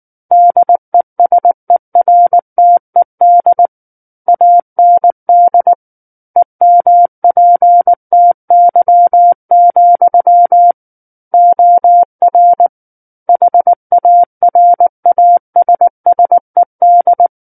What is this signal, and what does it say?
Morse code, heard as a keyed tone, about 19 wpm, 700 Hz. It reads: DESERTED AND EMPTY, OR HARASSED
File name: Book One / 10-Chapter 10 - 19 WPM